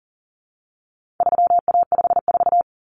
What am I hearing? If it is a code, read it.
Morse code, heard as a keyed tone, about 40 wpm, 700 Hz. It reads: 3A54
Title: Common Cut Numbers - 4-Digit 40wpm